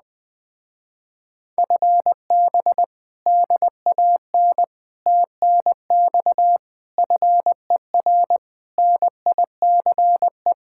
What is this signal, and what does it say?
Morse code, heard as a keyed tone, about 20 wpm, 700 Hz. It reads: FB DAN TNX FER NICE